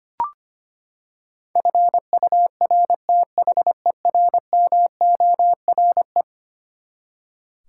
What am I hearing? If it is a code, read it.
Morse code, heard as a keyed tone, about 25 wpm, 700 Hz. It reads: FURTHERMORE
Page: morse-code-ninja-practice-25wpm